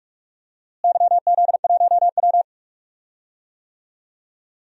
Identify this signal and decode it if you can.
Morse code, heard as a keyed tone, about 46 wpm, 690 Hz. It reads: YZ1W
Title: DXCC-100 Call Signs - 4 Characters - Repeated 3x - 45wpm